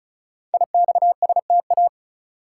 Morse code, heard as a keyed tone, about 36 wpm, 700 Hz. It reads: IXSTA
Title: Code Groups - 5 Letters - 35wpm